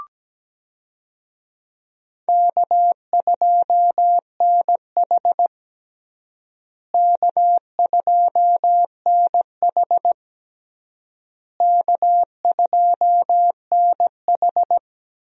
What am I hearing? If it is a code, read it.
Morse code, heard as a keyed tone, about 17 wpm, 700 Hz. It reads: K2NH K2NH K2NH
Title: USA Call Signs - 4 Characters - Repeated 3x - 17wpm